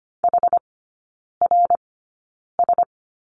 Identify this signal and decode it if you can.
Morse code, heard as a keyed tone, about 25 wpm, spaced as for 9 wpm, 700 Hz. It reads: HRS
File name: course / 25:10wpm / 093 Lesson 16e - Letter c - Three Characters 25wpm